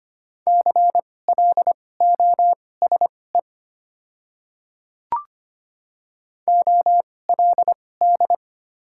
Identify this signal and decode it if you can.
Morse code, heard as a keyed tone, about 25 wpm, spaced as for 18 wpm, 700 Hz. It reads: CLOSE OLD
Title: 2x Character Spacing - Top 500 Words - 25wpm